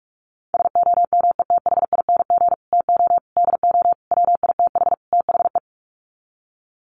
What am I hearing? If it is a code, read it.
Morse code, heard as a keyed tone, about 44 wpm, 700 Hz. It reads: SOMETHING TO DO WITH THE
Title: Sets of 5 Words Encore 45wpm